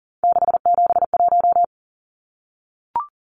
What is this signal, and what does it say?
Morse code, heard as a keyed tone, about 40 wpm, 700 Hz. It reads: 671
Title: Numbers - 3-Digit 40wpm